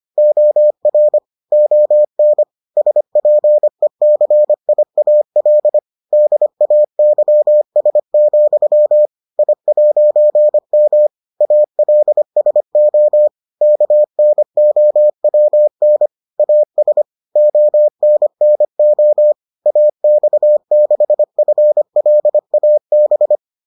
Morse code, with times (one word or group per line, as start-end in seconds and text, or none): 0.2-1.2 OR
1.5-2.4 ON
2.8-5.8 SPECIAL
6.1-9.1 DAYS,
9.4-11.1 I'M
11.4-13.3 ALSO
13.6-16.1 KNOWN
16.4-17.0 AS
17.4-19.3 ONNO
19.7-23.4 AX6FLAB